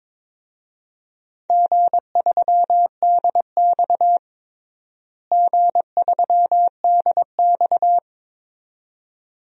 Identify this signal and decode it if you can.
Morse code, heard as a keyed tone, about 22 wpm, 700 Hz. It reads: G3DX G3DX
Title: DXCC-100 Call Signs - 4 Characters - Repeated 2x - 22wpm